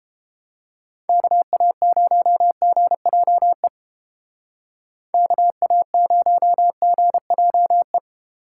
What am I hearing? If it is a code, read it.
Morse code, heard as a keyed tone, about 30 wpm, 700 Hz. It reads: KA0GJE KA0GJE